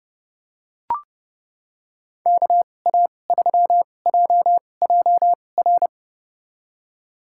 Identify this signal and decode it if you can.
Morse code, heard as a keyed tone, about 30 wpm, spaced as for 22 wpm, 700 Hz. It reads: KA3JJR
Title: USA Call Signs - 6 Characters - 2x Character Spacing - 30wpm